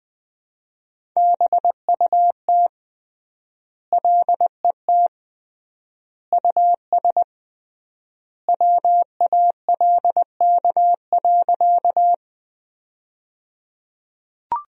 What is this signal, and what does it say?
Morse code, heard as a keyed tone, about 21 wpm, 700 Hz. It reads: BUT LET US WALK.
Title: 3x Word-Spacing - Sentences from Top 300 Words - 20wpm